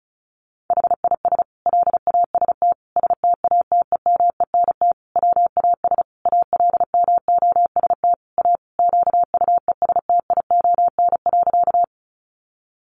Morse code, heard as a keyed tone, about 34 wpm, 700 Hz. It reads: HIS LAST STATEMENT WAS ALMOST A QUESTION.
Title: Sentences from Top 1200 Words 35wpm